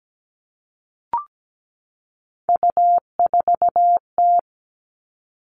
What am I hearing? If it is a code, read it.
Morse code, heard as a keyed tone, about 17 wpm, 700 Hz. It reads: U4T
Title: All Cut Numbers - 3-Digit 17wpm